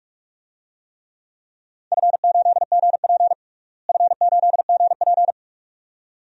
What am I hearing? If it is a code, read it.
Morse code, heard as a keyed tone, about 46 wpm, 700 Hz. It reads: F8GP F8GP
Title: DXCC-100 Call Signs - 4 Characters - Repeated 2x - 45wpm